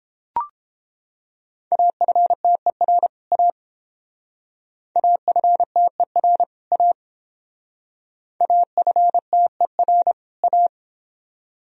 Morse code, heard as a keyed tone, about 30 wpm, 700 Hz. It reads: AFTER A AFTER A AFTER A